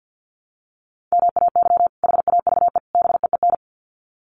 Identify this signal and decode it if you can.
Morse code, heard as a keyed tone, about 50 wpm, 700 Hz. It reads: MAY HAVE BEEN
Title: Sets of 3 Words 50wpm